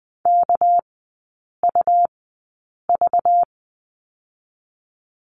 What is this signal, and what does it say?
Morse code, heard as a keyed tone, about 20 wpm, spaced as for 8 wpm, 700 Hz. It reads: KUV
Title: 198 Lesson 32e - Letter k - Three Characters 20wpm